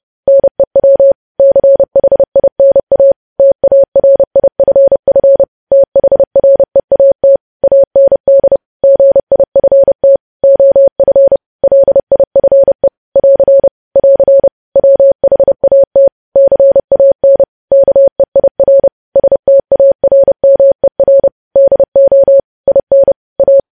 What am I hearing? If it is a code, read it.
Morse code, heard as a keyed tone, about 30 wpm, 550 Hz. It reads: NEW CHINA TARIFF THREAT AND GIFT OF LIFE <AR> <AR> WHAT CAN KEIR STARMER DO IN A
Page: news headlines in morse code at 30 wpm